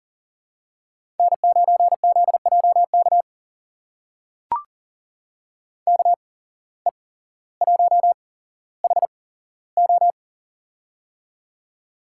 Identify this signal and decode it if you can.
Morse code, heard as a keyed tone, about 40 wpm, spaced as for 17 wpm, 700 Hz. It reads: N9ZJK KE1HO
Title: USA Call Signs - 5 Characters - 8x Character Spacing - 40wpm